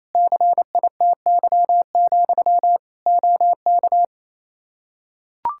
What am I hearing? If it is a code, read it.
Morse code, heard as a keyed tone, about 28 wpm, 700 Hz. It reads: CITY, OK
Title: Practice QSO - Exchange 1 - Beginning - 28wpm